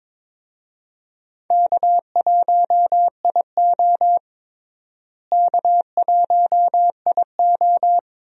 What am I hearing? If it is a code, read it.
Morse code, heard as a keyed tone, about 22 wpm, 700 Hz. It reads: K1IO K1IO
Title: USA Call Signs - 4 Characters - Repeated 2x - 22wpm